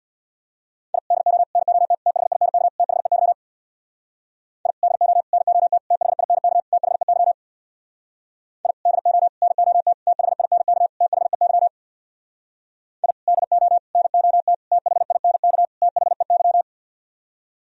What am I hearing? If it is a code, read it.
Morse code, heard as a keyed tone, about 60 wpm, 700 Hz. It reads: I DO NOT THINK THEY I DO NOT THINK THEY I DO NOT THINK THEY I DO NOT THINK THEY
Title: Speed-Racing - Sets of 5 Words 50wpm